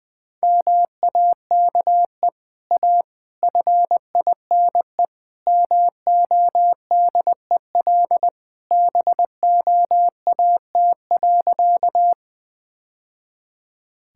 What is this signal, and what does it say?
Morse code, heard as a keyed tone, about 20 wpm, 700 Hz. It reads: MAKE A FINE MODEL BOAT.
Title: Sentences from Top 1000 Words - Review Part 5 - 20wpm